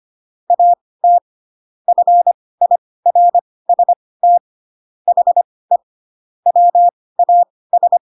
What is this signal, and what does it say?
Morse code, heard as a keyed tone, about 25 wpm, spaced as for 18 wpm, 700 Hz. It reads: AT FIRST HE WAS
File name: Pinocchio / Pinocchio ch7 12 25 ik1hge